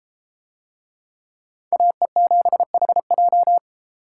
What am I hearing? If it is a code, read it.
Morse code, heard as a keyed tone, about 31 wpm, 700 Hz. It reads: AE7HJ